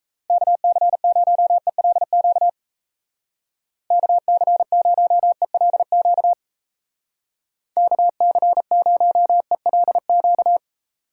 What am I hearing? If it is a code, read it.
Morse code, heard as a keyed tone, about 34 wpm, 700 Hz. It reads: KC0ELQ KC0ELQ KC0ELQ
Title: Speed Racing - USA 2x3 Call Signs 28wpm